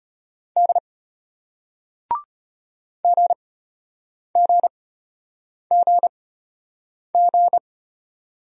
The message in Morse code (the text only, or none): D GGGG